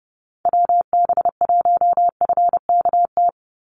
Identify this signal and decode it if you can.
Morse code, heard as a keyed tone, about 30 wpm, 700 Hz. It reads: WB1FKT
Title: USA Call Signs 30wpm